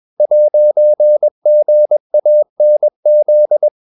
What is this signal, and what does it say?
Morse code, heard as a keyed tone, about 21 wpm, 600 Hz. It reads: 'GANZ